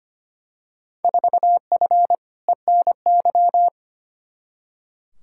Morse code, heard as a keyed tone, about 25 wpm, 700 Hz. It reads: 4F ENY
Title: Field Day Exchanges - 25wpm